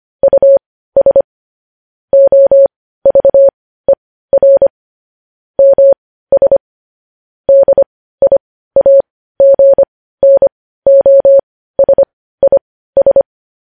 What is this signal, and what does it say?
Morse code, heard as a keyed tone, about 25 wpm, spaced as for 15 wpm, 550 Hz. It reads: US OVER MS DIAGNOSIS
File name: podcast / News headlines-1x-25x15